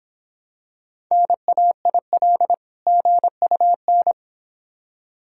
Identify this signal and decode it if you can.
Morse code, heard as a keyed tone, about 26 wpm, 700 Hz. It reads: NAIL GUN